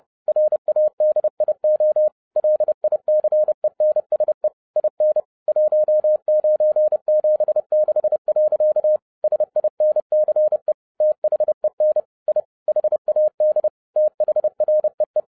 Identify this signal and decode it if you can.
Morse code, heard as a keyed tone, about 30 wpm, 610 Hz. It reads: RADIO LICENSE IN 1976. SINCE THEN I HAD THREE
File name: 2012-01-mp3 / n11-I DIDNT MISS THE SOLAR CYCLE 24-30wpm